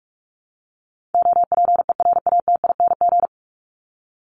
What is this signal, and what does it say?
Morse code, heard as a keyed tone, about 44 wpm, 700 Hz. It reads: OPERATING
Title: ING Suffix - Top 100 Words with Base Words 45wpm